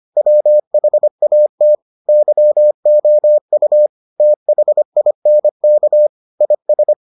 Morse code, heard as a keyed tone, about 25 wpm, 600 Hz. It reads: WHAT YOU THINK IS